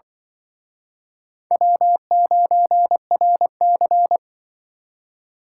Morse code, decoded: W9RC

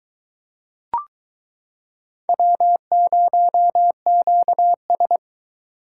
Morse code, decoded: W0QS

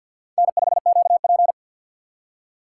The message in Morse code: N5QP

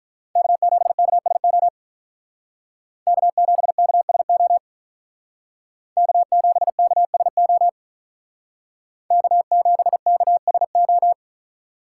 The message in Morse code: K7KSO K7KSO K7KSO K7KSO

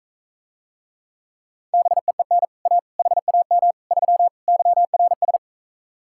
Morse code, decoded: BEEN A HAM 3 YRS

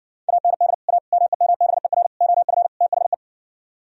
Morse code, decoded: AND A MEMBER OF THE

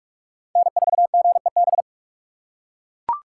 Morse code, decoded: N4GEB

